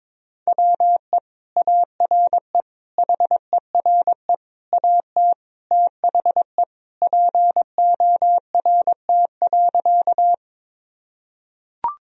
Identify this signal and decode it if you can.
Morse code, heard as a keyed tone, about 22 wpm, 700 Hz. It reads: WE ARE HERE AT THE PORT.
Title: Sentences from Top 1000 Words - Review Part 1 - 22wpm